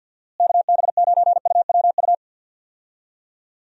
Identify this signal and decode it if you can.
Morse code, heard as a keyed tone, about 51 wpm, 700 Hz. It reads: KB9UWU
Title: USA Call Signs - 6 Characters - Repeated 2x - 50wpm